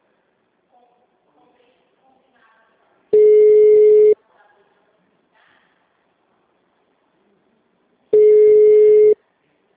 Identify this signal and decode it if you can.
Ringback tone (Europe, 425 Hz), with speech.
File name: out- -2022- - - .41428